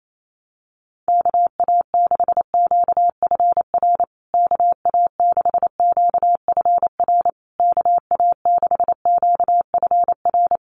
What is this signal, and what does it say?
Morse code, heard as a keyed tone, about 28 wpm, 700 Hz. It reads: KA6QFR KA6QFR KA6QFR
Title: USA Call Signs Repeated 3x 28wpm